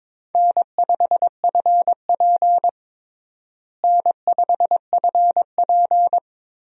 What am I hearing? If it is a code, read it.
Morse code, heard as a keyed tone, about 21 wpm, 700 Hz. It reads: N5FP N5FP
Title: USA Call Signs - 4 Characters - Repeated 2x - 22wpm